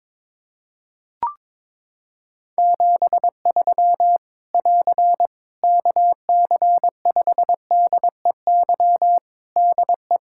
Morse code, heard as a keyed tone, about 22 wpm, 700 Hz. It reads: 73 <AR> KC5DEY DE
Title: Practice QSO - Exchange 4 - Ending - 22wpm